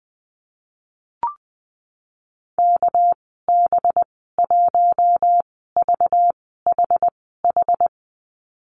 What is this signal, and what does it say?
Morse code, heard as a keyed tone, about 20 wpm, 700 Hz. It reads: K B 1 V H H
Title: 231 Lesson 37g - Letter x - Call Signs 20wpm